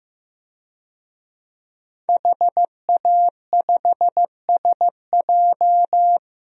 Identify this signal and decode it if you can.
Morse code, heard as a keyed tone, about 15 wpm, 700 Hz. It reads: HA5SJ
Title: DXCC-100 Call Signs - 5 Characters - Repeated 3x - 15wpm